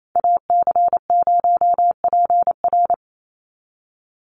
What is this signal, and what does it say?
Morse code, heard as a keyed tone, about 28 wpm, 700 Hz. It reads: AC0PR
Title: NA SOTA Activator Callsigns 28wpm